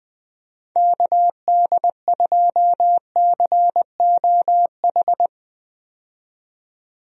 Morse code, heard as a keyed tone, about 21 wpm, 700 Hz. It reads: KD2COH
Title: USA Call Signs - 6 Characters - 20wpm